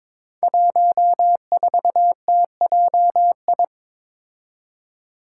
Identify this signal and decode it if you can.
Morse code, heard as a keyed tone, about 22 wpm, 700 Hz. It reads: 14TJI
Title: Five Challenging Letter-Number Combinations 22wpm